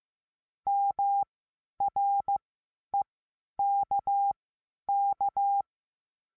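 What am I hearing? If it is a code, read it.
Morse code, heard as a keyed tone, about 15 wpm, spaced as for 10 wpm, 800 Hz. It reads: MREKK
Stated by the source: Morse code lesson adding the letter S.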